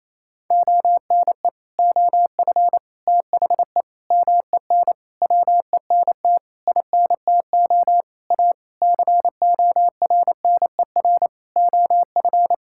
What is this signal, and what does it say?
Morse code, heard as a keyed tone, about 28 wpm, 700 Hz. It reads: ONE OF THE MEN WENT INTO A CORNER OF
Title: Sentences from Top 1600 Words 28wpm